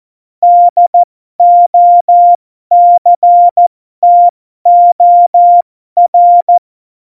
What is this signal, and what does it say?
Morse code, heard as a keyed tone, about 14 wpm, spaced as for 12 wpm, 700 Hz. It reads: DOCTOR
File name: Morsekurs / Witze